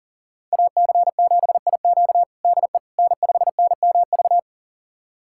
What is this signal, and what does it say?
Morse code, heard as a keyed tone, about 40 wpm, 700 Hz. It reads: AC7IQ DE N5NMV
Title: USA Callsign DE Callsign - 5 Characters - 40wpm